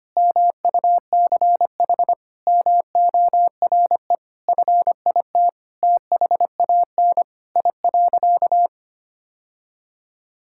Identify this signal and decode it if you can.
Morse code, heard as a keyed tone, about 25 wpm, 700 Hz. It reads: MUCH MORE FIT THAN I.
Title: Sentences from Top 1000 Words - Review Part 5 - 25wpm